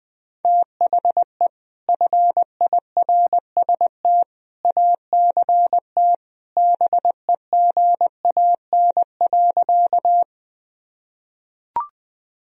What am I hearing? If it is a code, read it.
Morse code, heard as a keyed tone, about 21 wpm, 700 Hz. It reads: THE FIRST ACT BEGAN.
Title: Sentences from Top 1000 Words - Review Part 1 - 20wpm